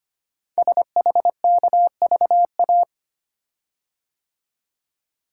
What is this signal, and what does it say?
Morse code, heard as a keyed tone, about 25 wpm, 700 Hz. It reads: SHKVA
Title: Code Groups - 5 Letters - 25wpm